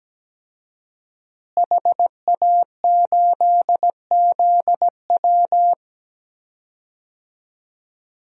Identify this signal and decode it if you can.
Morse code, heard as a keyed tone, about 17 wpm, 700 Hz. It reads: HA8ZW
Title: DXCC-100 Call Signs - 5 Characters - Repeated 2x - 17wpm